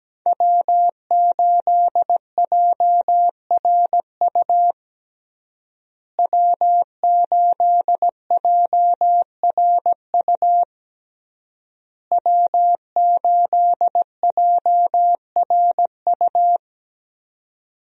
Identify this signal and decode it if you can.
Morse code, heard as a keyed tone, about 17 wpm, 700 Hz. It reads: W8JRU W8JRU W8JRU
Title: USA Call Signs - 5 Characters - Repeated 3x - 17wpm